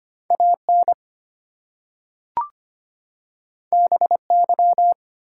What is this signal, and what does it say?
Morse code, heard as a keyed tone, about 25 wpm, 700 Hz. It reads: AN BY
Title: Two Letter Words - 25wpm